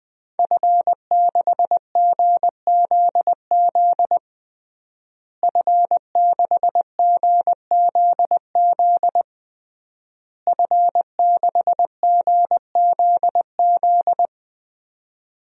Morse code, heard as a keyed tone, about 21 wpm, 700 Hz. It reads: F6GZZ F6GZZ F6GZZ
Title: DXCC-100 Call Signs - 5 Characters - Repeated 3x - 20wpm